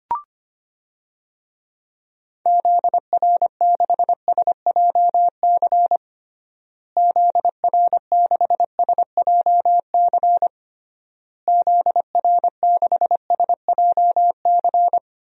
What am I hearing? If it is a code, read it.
Morse code, heard as a keyed tone, about 25 wpm, 700 Hz. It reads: ZR6SJC ZR6SJC ZR6SJC
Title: DXCC-100 Call Signs - 6 Characters - Repeated 3x - 25wpm